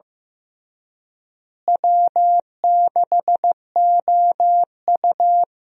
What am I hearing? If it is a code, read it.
Morse code, heard as a keyed tone, about 15 wpm, 700 Hz. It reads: W6OU
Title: USA Call Signs - 4 Characters - Repeated 3x - 15wpm